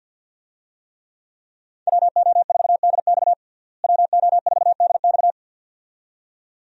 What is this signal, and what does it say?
Morse code, heard as a keyed tone, about 49 wpm, 700 Hz. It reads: WO4DX WO4DX
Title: USA Call Signs - 5 Characters - Repeated 2x - 50wpm